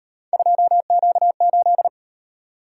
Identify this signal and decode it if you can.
Morse code, heard as a keyed tone, about 37 wpm, 700 Hz. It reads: 2Q8